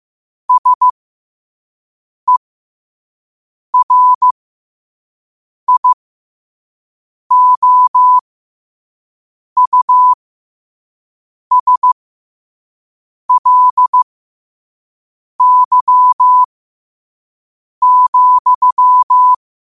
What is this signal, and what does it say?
Morse code, heard as a keyed tone, about 15 wpm, spaced as for 5 wpm, 1000 Hz. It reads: SERIOUSLY,
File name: qotdfiles / 5wpm